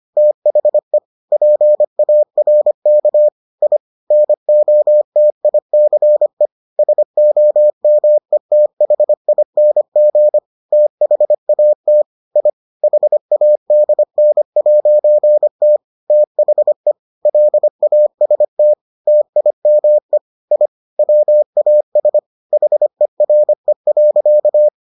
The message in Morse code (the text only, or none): THE PARK I NOTICE SOMETHING THAT I HADN'T THE LAST TIME I WAS HERE.